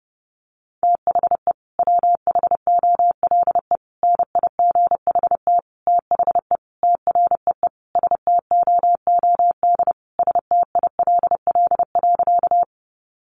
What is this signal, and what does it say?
Morse code, heard as a keyed tone, about 30 wpm, 700 Hz. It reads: THE WHOLE NIGHT THE TREE STOOD STILL.
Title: Sentences from Top 500 Words 30wpm